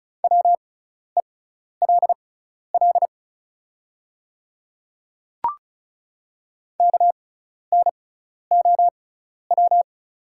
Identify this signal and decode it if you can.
Morse code, heard as a keyed tone, about 34 wpm, spaced as for 12 wpm, 700 Hz. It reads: WELL KNOW
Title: 6x Character Spacing - Top 200 Words - 35wpm